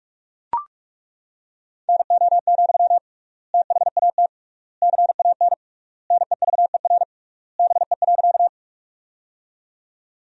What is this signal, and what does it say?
Morse code, heard as a keyed tone, about 41 wpm, 700 Hz. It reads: NO, THAT CAN NEVER BE.
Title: 3x Word-Spacing - Sentences from Top 300 Words - 45wpm